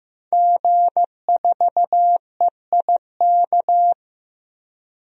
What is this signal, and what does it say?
Morse code, heard as a keyed tone, about 15 wpm, 700 Hz. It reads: G4EIK
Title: DXCC-100 Call Signs - 5 Characters - Repeated 2x - 15wpm